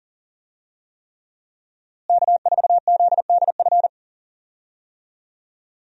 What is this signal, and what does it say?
Morse code, heard as a keyed tone, about 40 wpm, 700 Hz. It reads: K4ZDF
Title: USA Call Signs - 5 Characters - Repeated 3x - 40wpm